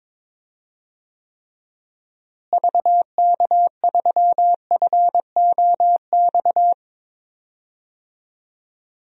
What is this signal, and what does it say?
Morse code, heard as a keyed tone, about 23 wpm, 700 Hz. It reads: VK3FOX
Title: DXCC-100 Call Signs - 6 Characters - Repeated 3x - 22wpm